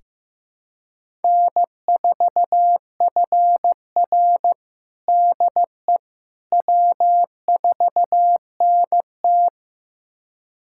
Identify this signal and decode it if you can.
Morse code, heard as a keyed tone, about 15 wpm, 700 Hz. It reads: N4FR DE W4NT
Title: USA Callsign DE Callsign - 4 Characters - 15wpm